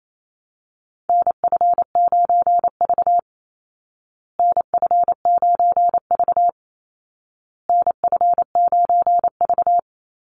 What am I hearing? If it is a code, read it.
Morse code, heard as a keyed tone, about 28 wpm, 700 Hz. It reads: NF9V NF9V NF9V
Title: Call Signs Repeated 2x - 28wpm